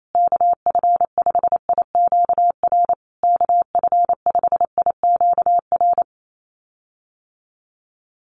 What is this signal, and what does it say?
Morse code, heard as a keyed tone, about 28 wpm, 700 Hz. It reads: KF5IQR KF5IQR
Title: USA Call Signs Repeated 3x 28wpm